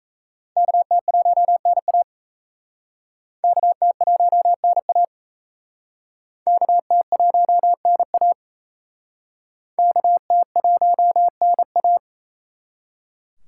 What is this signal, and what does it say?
Morse code, heard as a keyed tone, about 34 wpm, 700 Hz. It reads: KT1NA KT1NA KT1NA KT1NA
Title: Speed Racing - USA 2x2 Call Signs 28wpm